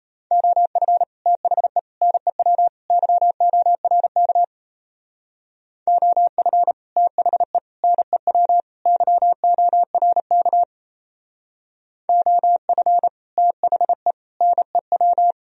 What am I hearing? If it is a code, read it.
Morse code, heard as a keyed tone, about 35 wpm, 700 Hz. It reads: OF THE NEW YORK OF THE NEW YORK OF THE NEW